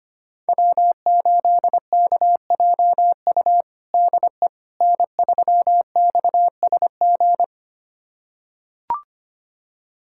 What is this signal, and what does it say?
Morse code, heard as a keyed tone, about 25 wpm, 700 Hz. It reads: W8KJU DE N3XSG
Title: USA Callsign DE Callsign - 5 Characters - 25wpm